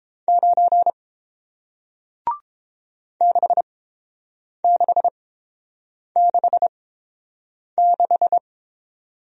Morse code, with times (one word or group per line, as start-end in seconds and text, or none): 0.3-0.9 9
3.2-8.4 6666